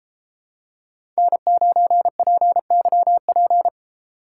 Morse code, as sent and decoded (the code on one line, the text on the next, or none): -. ----. .--. -.-- .--.
N9PYP